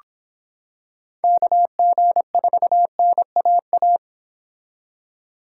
Morse code, heard as a keyed tone, about 25 wpm, 700 Hz. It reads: KG4NAA